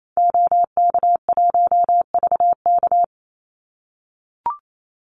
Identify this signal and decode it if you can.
Morse code, heard as a keyed tone, about 28 wpm, 700 Hz. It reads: OK1VK
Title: Call Signs - 28wpm